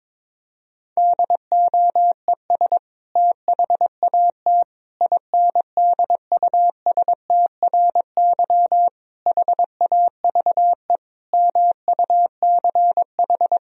Morse code, heard as a keyed tone, about 22 wpm, 700 Hz. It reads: DOES THAT INDUSTRY HAVE MUCH
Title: Sentences from Top 1000 Words - Review Part 4 - 22wpm